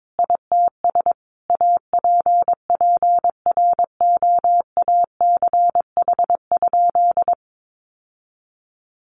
Morse code, heard as a keyed tone, about 22 wpm, 700 Hz. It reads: ITS APPROACH?
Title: Sentences from Top 1500 Words 22wpm